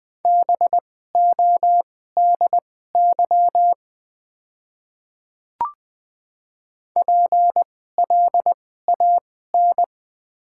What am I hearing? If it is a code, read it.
Morse code, heard as a keyed tone, about 20 wpm, spaced as for 14 wpm, 690 Hz. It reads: BODY PLAN
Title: 2x Character Spacing - Top 500 Words - 20wpm